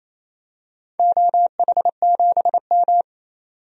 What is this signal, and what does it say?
Morse code, heard as a keyed tone, about 28 wpm, 700 Hz. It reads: OH7M
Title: DXCC-100 Call Signs - 4 Characters - 28wpm